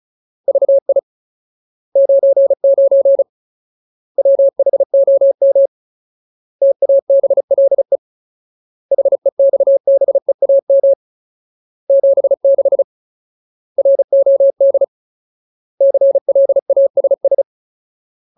Morse code, heard as a keyed tone, about 34 wpm, 550 Hz. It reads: VI 99 WHOM TABLE HEXBEAM 76 ROD CLASS
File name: Ultimate Word List / 35WPM Ultimate